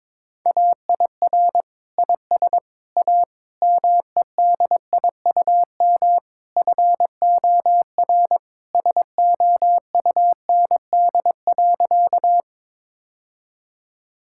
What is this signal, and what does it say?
Morse code, heard as a keyed tone, about 22 wpm, 700 Hz. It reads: AIR IS A MEDIUM FOR SOUND.
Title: Sentences from Top 1000 Words - Review Part 6 - 22wpm